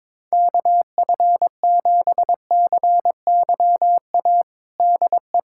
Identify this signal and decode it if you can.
Morse code, heard as a keyed tone, about 22 wpm, 700 Hz. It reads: KF7CYA DE
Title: USA Callsign DE Callsign - 6 Characters - 22wpm